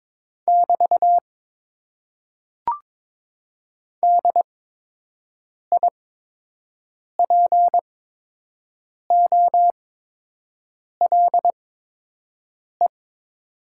Morse code, heard as a keyed tone, about 22 wpm, spaced as for 6 wpm, 700 Hz. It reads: <BT> DIPOLE
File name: Top 90 QSO Elements - 8x Character Spacing - 22wpm